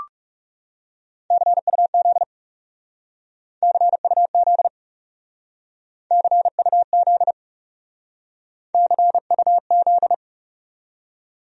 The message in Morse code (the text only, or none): CUZ CUZ CUZ CUZ